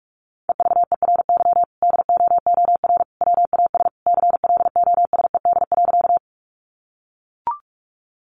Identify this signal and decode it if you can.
Morse code, heard as a keyed tone, about 44 wpm, 700 Hz. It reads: EVERY DOOR WAS CLOSED.
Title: Sentences from Top 1400 Words 45wpm